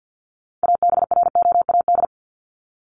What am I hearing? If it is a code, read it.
Morse code, heard as a keyed tone, about 50 wpm, 700 Hz. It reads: ABROAD
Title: Top 1751-2000 Words 50wpm